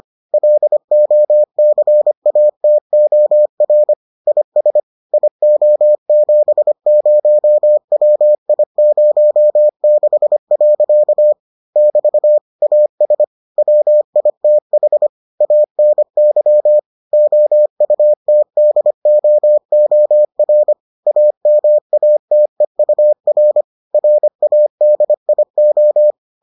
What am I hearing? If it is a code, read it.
Morse code, heard as a keyed tone, about 25 wpm, 600 Hz. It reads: LOCATOR IS IO70WI06. <BT> AS WITH ANY OUTDOOR AMATEUR RADIO